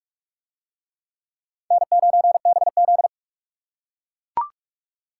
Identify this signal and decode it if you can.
Morse code, heard as a keyed tone, about 39 wpm, 700 Hz. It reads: N9BZ